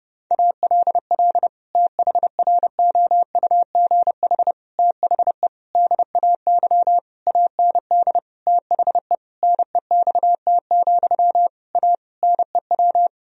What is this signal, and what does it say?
Morse code, heard as a keyed tone, about 30 wpm, 700 Hz. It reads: ALL THROUGH THE DAY AND THE NEXT, A NEW
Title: Sentences from Top 1000 Words - Review Part 1 - 30wpm